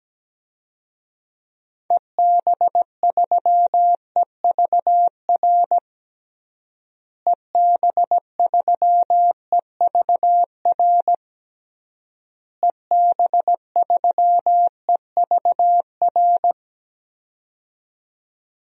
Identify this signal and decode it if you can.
Morse code, heard as a keyed tone, about 17 wpm, 700 Hz. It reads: EB3EVR EB3EVR EB3EVR
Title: DXCC-100 Call Signs - 6 Characters - Repeated 3x - 17wpm